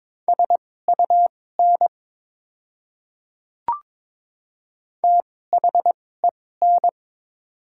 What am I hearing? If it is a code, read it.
Morse code, heard as a keyed tone, about 22 wpm, spaced as for 16 wpm, 700 Hz. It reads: SUN THEN
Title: 2x Character Spacing - Top 300 Words - 22wpm